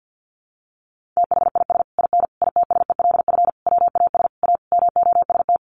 Morse code, heard as a keyed tone, about 50 wpm, 700 Hz. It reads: THIS IN ITSELF WAS A MOST
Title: Sentences from Top 1400 Words 50wpm